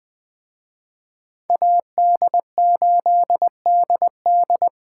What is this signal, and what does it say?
Morse code, heard as a keyed tone, about 20 wpm, 700 Hz. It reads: AD8DD
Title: Speed Racing - USA 2x2 Call Signs 17wpm